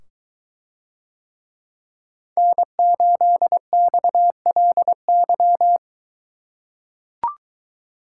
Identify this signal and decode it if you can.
Morse code, heard as a keyed tone, about 23 wpm, 700 Hz. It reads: N8XLY